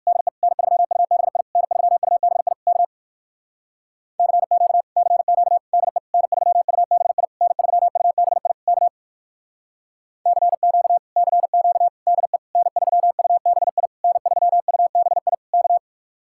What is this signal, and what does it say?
Morse code, heard as a keyed tone, about 51 wpm, 700 Hz. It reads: DE N3UBI N3UBI K CQ CQ DE N3UBI N3UBI K CQ CQ DE N3UBI N3UBI K
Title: Speed-Racing Common QSO Phrases Encore - 45wpm